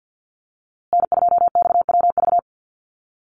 Morse code, heard as a keyed tone, about 50 wpm, 700 Hz. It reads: N2XWV
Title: USA Call Signs Repeated 3x 50wpm